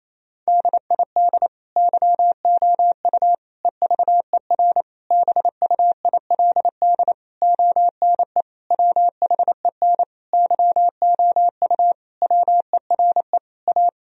Morse code, heard as a keyed tone, about 28 wpm, 700 Hz. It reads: DID YOU EVER BUILD ONE WHEN YOU WERE A
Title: Sentences from Top 1000 Words - Review Part 1 - 28wpm